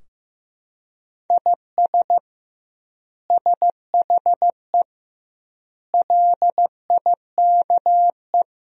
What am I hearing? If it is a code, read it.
Morse code, heard as a keyed tone, about 15 wpm, 700 Hz. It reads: IS SHE LIKE
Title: 2x Word-Spacing - Sentences from Top 100 Words - 15wpm